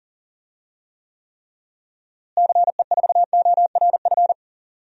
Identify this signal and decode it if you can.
Morse code, heard as a keyed tone, about 40 wpm, 700 Hz. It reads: CE4ORF
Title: DXCC-100 Call Signs - 6 Characters - Repeated 3x - 40wpm